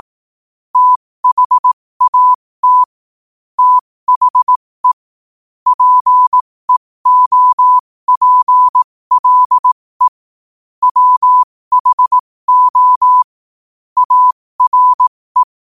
Morse code, heard as a keyed tone, about 18 wpm, spaced as for 15 wpm, 1000 Hz. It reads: THAT THE PEOPLE WHO ARE
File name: qotdfiles / 15wpm